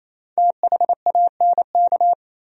Morse code, heard as a keyed tone, about 38 wpm, 700 Hz. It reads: THANK